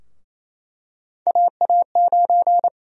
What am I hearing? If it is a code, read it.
Morse code, heard as a keyed tone, about 27 wpm, 700 Hz. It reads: AA9
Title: USA Call Sign Prefixes - Repeated 2x - 28wpm